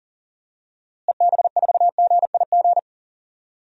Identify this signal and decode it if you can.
Morse code, heard as a keyed tone, about 40 wpm, 700 Hz. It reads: EB4GIG
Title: DXCC-100 Call Signs - 6 Characters - Repeated 2x - 40wpm